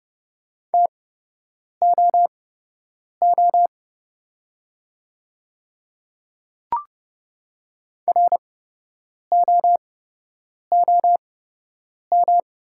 Morse code, heard as a keyed tone, about 30 wpm, spaced as for 8 wpm, 700 Hz. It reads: TOO ROOM
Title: 8x Character Spacing - Top 300 Words - 30wpm